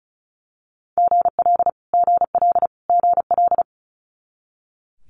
Morse code, heard as a keyed tone, about 36 wpm, 700 Hz. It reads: GL GL GL
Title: Multi-letter Phonemes Repeated 3x 35wpm